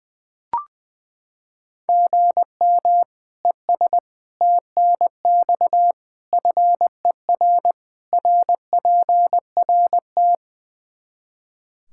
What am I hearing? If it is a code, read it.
Morse code, heard as a keyed tone, about 21 wpm, 700 Hz. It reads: GM ES TNX FER RPRT
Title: K1USN SST QSO Practice 20wpm